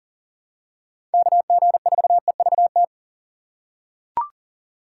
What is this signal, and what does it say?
Morse code, heard as a keyed tone, about 40 wpm, 700 Hz. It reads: KG4EVT
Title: USA Call Signs - 6 Characters - 6x Character Spacing - 40wpm